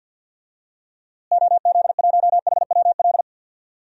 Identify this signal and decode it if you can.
Morse code, heard as a keyed tone, about 52 wpm, 700 Hz. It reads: OZ1HWL
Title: DXCC-100 Call Signs - 6 Characters - Repeated 3x - 50wpm